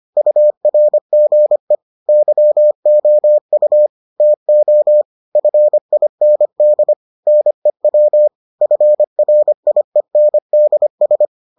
Morse code, heard as a keyed tone, about 25 wpm, 600 Hz. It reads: URGE YOU TO FIND NEW FRIENDS